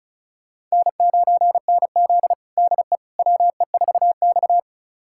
Morse code, heard as a keyed tone, about 34 wpm, 700 Hz. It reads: N9NZ DE WE4X
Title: USA Callsign DE Callsign - 4 Characters - 35wpm